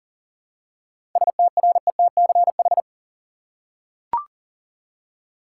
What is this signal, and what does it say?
Morse code, heard as a keyed tone, about 39 wpm, 700 Hz. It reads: STRETCH
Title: Seven Letter Words - 40wpm